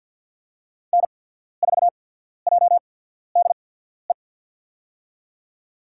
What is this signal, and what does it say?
Morse code, heard as a keyed tone, about 53 wpm, spaced as for 14 wpm, 700 Hz. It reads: N4JDE
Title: USA Call Signs - 5 Characters - 8x Character Spacing - 50wpm